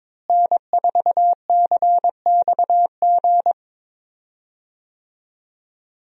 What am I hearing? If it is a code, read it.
Morse code, heard as a keyed tone, about 21 wpm, 700 Hz. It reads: N4CXG
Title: USA Call Signs - 5 Characters - Repeated 3x - 22wpm